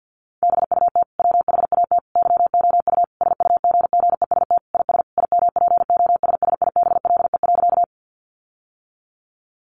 Morse code, heard as a keyed tone, about 50 wpm, 700 Hz. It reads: BUT WHAT YOU SUGGEST IS IMPOSSIBLE.
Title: Sentences from Top 1300 Words 50wpm